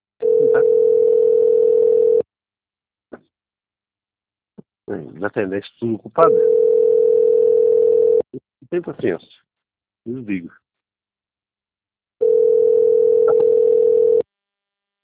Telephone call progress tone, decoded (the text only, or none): ringback tone